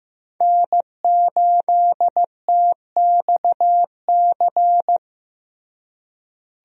Morse code, heard as a keyed tone, about 15 wpm, 700 Hz. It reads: N8TXC
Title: USA Call Signs - 5 Characters - Repeated 2x - 15wpm